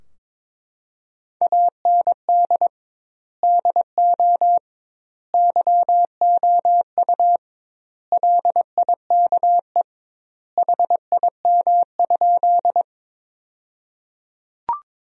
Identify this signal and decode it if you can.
Morse code, heard as a keyed tone, about 21 wpm, 700 Hz. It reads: AND DO YOU LIKE HIM?
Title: 2x Word-Spacing - Sentences from Top 100 Words - 22wpm